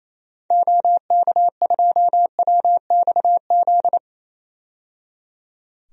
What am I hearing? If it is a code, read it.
Morse code, heard as a keyed tone, about 27 wpm, 700 Hz. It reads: OK2WXZ
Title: DXCC-100 Call Signs - 6 Characters - 28wpm